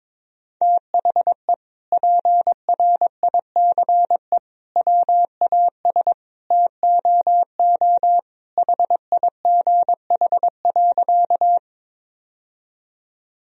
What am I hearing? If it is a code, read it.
Morse code, heard as a keyed tone, about 22 wpm, 700 Hz. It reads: THE PRICE WAS TOO HIGH.
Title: Sentences from Top 1000 Words - Review Part 4 - 22wpm